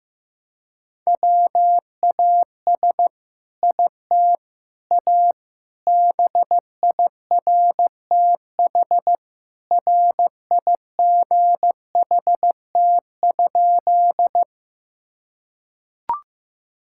Morse code, heard as a keyed tone, about 15 wpm, 700 Hz. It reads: WAS IT A BIRTH RIGHT?
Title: Sentences from Top 1000 Words - Review Part 5 - 15wpm